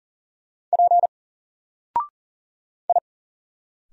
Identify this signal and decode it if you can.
Morse code, heard as a keyed tone, about 42 wpm, 700 Hz. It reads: P I
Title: Single Character (Letter:Number:Punctuation:Prosign) 40wpm